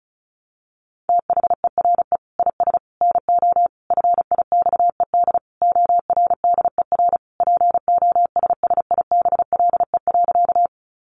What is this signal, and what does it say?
Morse code, heard as a keyed tone, about 34 wpm, 700 Hz. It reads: THERE IS NO FIXED ORDER POSSIBLE.
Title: Sentences from Top 1100 Words 35wpm